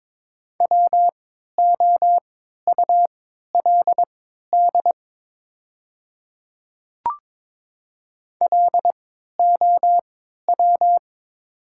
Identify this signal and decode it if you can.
Morse code, heard as a keyed tone, about 22 wpm, spaced as for 12 wpm, 700 Hz. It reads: WOULD LOW
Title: 3x Character Spacing - Top 500 Words - 22wpm